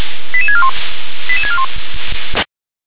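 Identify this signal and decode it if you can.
Five-tone selective call (ZVEI), heard as quick runs of five tones, tones of about 70 ms. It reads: 88641 88641
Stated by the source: ZVEI five-tone selective call received on SDR.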